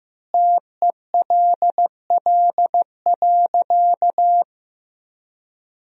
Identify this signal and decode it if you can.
Morse code, heard as a keyed tone, about 16 wpm, 700 Hz. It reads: TELL.
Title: 2x Word-Spacing - Sentences from Top 300 Words - 15wpm